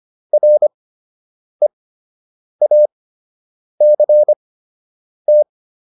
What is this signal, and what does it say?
Morse code, heard as a keyed tone, about 25 wpm, spaced as for 8 wpm, 600 Hz. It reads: REACT